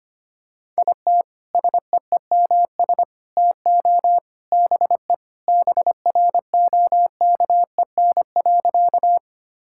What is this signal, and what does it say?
Morse code, heard as a keyed tone, about 25 wpm, 700 Hz. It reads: IT SEEMS TO BE BROKEN.
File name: practice / 25wpm / Sentences with 3-Word Phrases - 25wpm